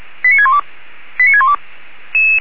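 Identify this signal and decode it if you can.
Five-tone selective call (ZVEI), heard as quick runs of five tones, tones of about 70 ms. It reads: 89612 89612
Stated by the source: ZVEI five-tone selective call 89247, received with an SDR, archived by Feuerwehr Orlamünde.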